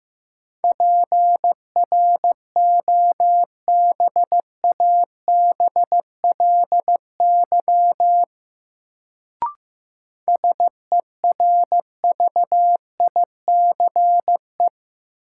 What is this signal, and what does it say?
Morse code, heard as a keyed tone, about 15 wpm, 700 Hz. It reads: PROBABLY SERVICE
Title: Sentences from Top 700 Words - New Word Drill - 15wpm